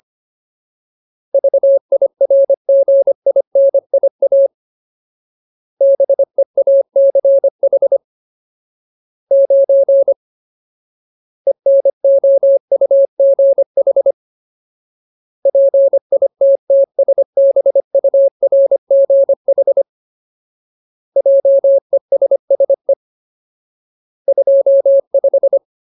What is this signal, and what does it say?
Morse code, heard as a keyed tone, about 25 wpm, 550 Hz. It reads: VIRGINIA BEACH 9 ENOUGH PITTSBURGH JESSE 25
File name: Ultimate Word List / 25WPM Ultimate